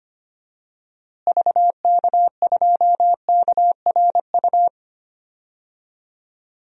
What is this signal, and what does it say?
Morse code, heard as a keyed tone, about 25 wpm, 700 Hz. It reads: VK2KRU
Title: DXCC-100 Call Signs - 6 Characters - Repeated 2x - 25wpm